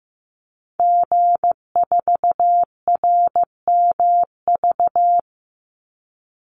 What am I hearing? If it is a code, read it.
Morse code, heard as a keyed tone, about 15 wpm, 700 Hz. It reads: G4RMV
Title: Call Signs Repeated 3x - 15wpm